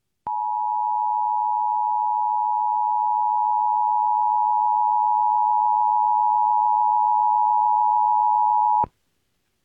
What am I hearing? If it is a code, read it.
Emergency Alert System (SAME) tones (the 853 and 960 Hz attention tone). It is heard from 0.2 to 8.9 s.